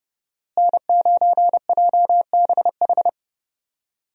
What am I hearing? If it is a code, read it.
Morse code, heard as a keyed tone, about 32 wpm, 700 Hz. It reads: N9JBH
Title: USA Call Signs - 5 Characters - Repeated 2x - 30wpm